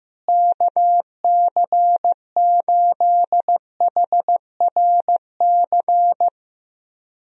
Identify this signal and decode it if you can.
Morse code, heard as a keyed tone, about 15 wpm, 700 Hz. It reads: KC8HRC
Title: USA Call Signs - 6 Characters - Repeated 3x - 15wpm